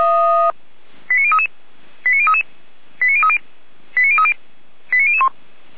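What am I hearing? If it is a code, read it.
Five-tone selective call (ZVEI), heard as quick runs of five tones, tones of about 70 ms. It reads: 89033 89033 89030 89030 89001
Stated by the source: ZVEI five-tone selective call received on an SDR, from the Feuerwehr Orlamünde fire brigade audio archive.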